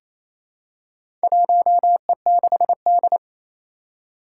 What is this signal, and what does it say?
Morse code, heard as a keyed tone, about 28 wpm, 700 Hz. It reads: 1E6D
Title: Four Challenging Letter-Number Combinations 28wpm